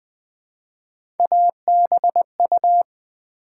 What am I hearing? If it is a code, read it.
Morse code, heard as a keyed tone, about 20 wpm, 700 Hz. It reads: ABU
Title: Build A Word - 1 Letter At A Time - 5-Letter Words - 20wpm